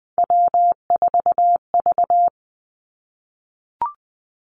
Morse code, heard as a keyed tone, about 20 wpm, 700 Hz. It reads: W4V
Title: Three Challenging Letter-Number Combinations 20wpm